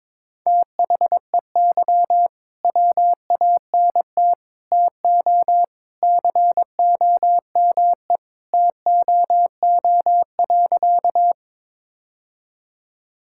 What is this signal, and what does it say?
Morse code, heard as a keyed tone, about 22 wpm, 700 Hz. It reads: THEY WANT TO COME TOO.
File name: practice / 22wpm / Sentences with 3-Word Phrases - 22wpm